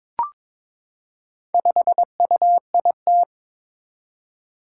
Morse code, heard as a keyed tone, about 22 wpm, 700 Hz. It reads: 5UIT
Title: Four Challenging Letter-Number Combinations 22wpm